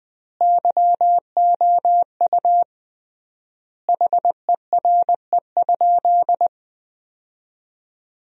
Morse code, heard as a keyed tone, about 20 wpm, 700 Hz. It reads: YOU HERE?
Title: 3x Word-Spacing - Sentences from Top 500 Words - 20wpm